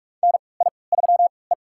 Morse code, heard as a keyed tone, about 46 wpm, spaced as for 26 wpm, 690 Hz. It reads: NI3E
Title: USA Call Signs - 4 Characters - 3x Character Spacing - 45wpm